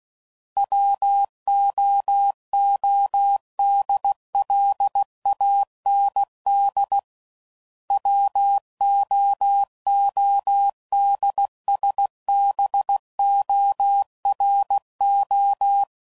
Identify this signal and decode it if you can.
Morse code, heard as a keyed tone, about 16 wpm, 800 Hz. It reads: WOODLAND WOODSBORO
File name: CW Code Groups @15WPM / fromSM's / Cities